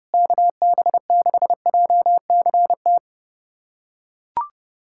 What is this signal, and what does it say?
Morse code, heard as a keyed tone, about 32 wpm, 700 Hz. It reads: KB6JCT
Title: USA Callsign DE Callsign - 6 Characters - 30wpm